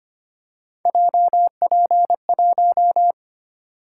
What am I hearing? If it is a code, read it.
Morse code, heard as a keyed tone, about 25 wpm, 700 Hz. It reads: JP1
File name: Three Letter-Number Combinations - 25wpm